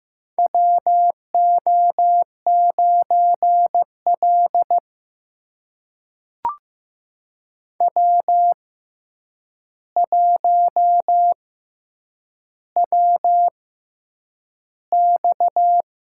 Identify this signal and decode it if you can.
Morse code, heard as a keyed tone, about 15 wpm, spaced as for 8 wpm, 700 Hz. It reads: WO9L W1WX
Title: USA Call Signs - 4 Characters - 6x Character Spacing - 15wpm